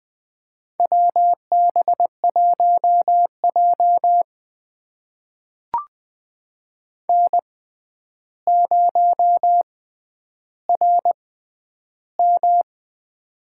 Morse code, heard as a keyed tone, about 20 wpm, spaced as for 10 wpm, 700 Hz. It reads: WB1J N0RM
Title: USA Call Signs - 4 Characters - 6x Character Spacing - 20wpm